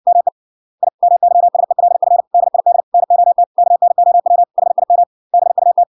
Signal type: Morse code